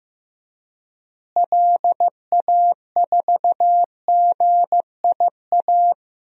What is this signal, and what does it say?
Morse code, heard as a keyed tone, about 15 wpm, 700 Hz. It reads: LA4GIA
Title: DXCC-100 Call Signs - 6 Characters - Repeated 2x - 15wpm